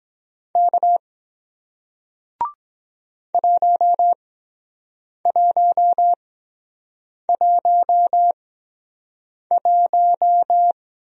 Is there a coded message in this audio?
Yes: Morse code from 0.6 to 10.7 s, heard as a keyed tone, about 23 wpm, spaced as for 7 wpm, 700 Hz. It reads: K 1111